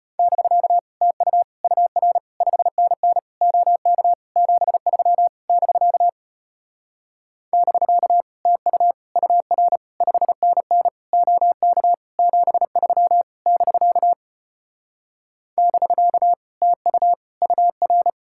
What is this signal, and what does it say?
Morse code, heard as a keyed tone, about 33 wpm, 700 Hz. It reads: <BK> TU UR 5NN OK 73 <BK> <BK> TU UR 5NN OK 73 <BK> <BK> TU UR